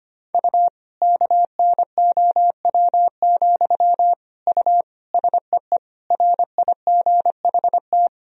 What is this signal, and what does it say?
Morse code, heard as a keyed tone, about 25 wpm, 700 Hz. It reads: U KNOW, U SEE RIGHT
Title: Sentences from Top 200 Words and Abbreviations 25wpm